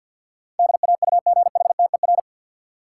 Morse code, heard as a keyed tone, about 53 wpm, 700 Hz. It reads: DAUGHTER
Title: Sentences from Top 1500 Words - New Word Drill Alt - 50wpm